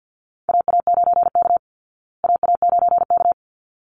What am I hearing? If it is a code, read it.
Morse code, heard as a keyed tone, about 48 wpm, 700 Hz. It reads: AA9K AA9K
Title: Call Signs Repeated 3x - 50wpm